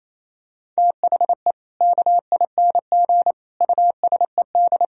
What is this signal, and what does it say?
Morse code, heard as a keyed tone, about 28 wpm, 700 Hz. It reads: THE KING USED
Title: Sentences from Top 1000 Words - Review Part 4 - 28wpm